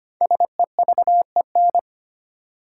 Morse code, heard as a keyed tone, about 25 wpm, 700 Hz. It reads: SEVEN
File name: 5 Letter Words - First 3 Letters Given - 25wpm